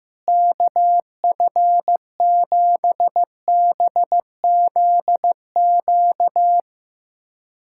Morse code, heard as a keyed tone, about 15 wpm, 700 Hz. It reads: KF7BZQ
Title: USA Callsign DE Callsign - 6 Characters - 15wpm